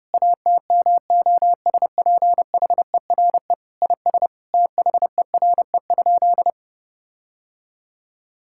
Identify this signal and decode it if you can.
Morse code, heard as a keyed tone, about 30 wpm, 700 Hz. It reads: ATMOSPHERE IS THERE?
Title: Sentences from Top 1600 Words 30wpm